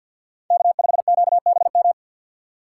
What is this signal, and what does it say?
Morse code, heard as a keyed tone, about 51 wpm, 700 Hz. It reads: K5QBM
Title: USA Call Signs - 5 Characters - 4x Character Spacing - 50wpm